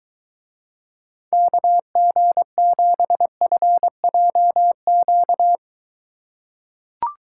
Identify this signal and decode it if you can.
Morse code, heard as a keyed tone, about 23 wpm, 700 Hz. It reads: KG7FJQ